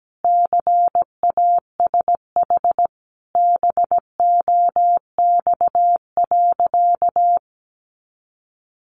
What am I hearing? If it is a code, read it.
Morse code, heard as a keyed tone, about 17 wpm, 700 Hz. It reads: CASH BOX.
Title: Sentences from Top 700 Words 17wpm